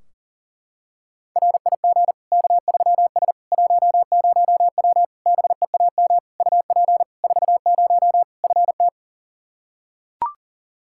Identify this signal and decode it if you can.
Morse code, heard as a keyed tone, about 37 wpm, 700 Hz. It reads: RIG K3S 10W BEAM UP 40 FT
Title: Practice QSO - Exchange 2 - Info - 40wpm